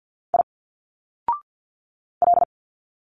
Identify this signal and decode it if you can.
Morse code, heard as a keyed tone, about 50 wpm, 700 Hz. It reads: I L
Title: Single Letters 50wpm